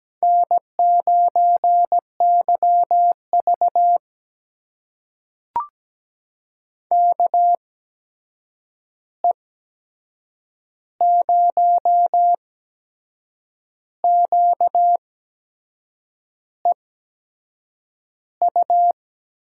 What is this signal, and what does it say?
Morse code, heard as a keyed tone, about 17 wpm, spaced as for 6 wpm, 700 Hz. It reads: N9YV KE0QEU
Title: USA Call Signs - 8x Character Spacing - 17wpm